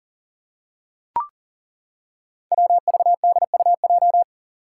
Morse code, heard as a keyed tone, about 38 wpm, 700 Hz. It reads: WVDUJ
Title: Five Challenging Letter-Number Combinations 40wpm